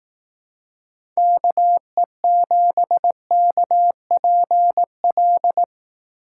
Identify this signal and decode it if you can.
Morse code, heard as a keyed tone, about 18 wpm, 700 Hz. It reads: KE7KPL
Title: Speed Racing - USA Call Signs 15wpm